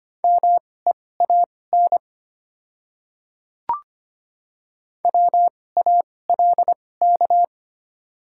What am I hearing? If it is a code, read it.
Morse code, heard as a keyed tone, about 25 wpm, spaced as for 18 wpm, 690 Hz. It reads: MEAN WALK
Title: 2x Character Spacing - Top 300 Words - 25wpm